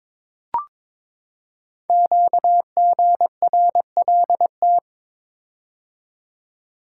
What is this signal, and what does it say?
Morse code, heard as a keyed tone, about 22 wpm, 700 Hz. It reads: QGRLT
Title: Code Groups - 5 Letters - 22wpm